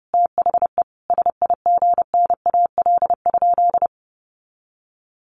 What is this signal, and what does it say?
Morse code, heard as a keyed tone, about 28 wpm, 700 Hz. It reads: THE SIGNAL?
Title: Sentences from Top 1400 Words 30wpm